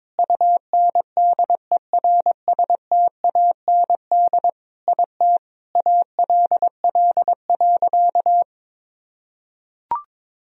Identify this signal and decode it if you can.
Morse code, heard as a keyed tone, about 22 wpm, 700 Hz. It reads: UNDERSTAND IT ALL.
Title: Sentences from Top 1000 Words - Review Part 4 - 22wpm